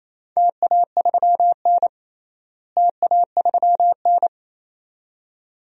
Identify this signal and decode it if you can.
Morse code, heard as a keyed tone, about 27 wpm, 700 Hz. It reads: TA3N TA3N
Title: DXCC-100 Call Signs - 4 Characters - Repeated 2x - 28wpm